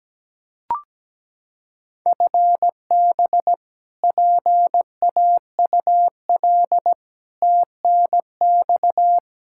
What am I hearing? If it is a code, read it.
Morse code, heard as a keyed tone, about 17 wpm, 700 Hz. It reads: FB PAUL TNX
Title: Practice QSO - Exchange 3 - More Info - 17wpm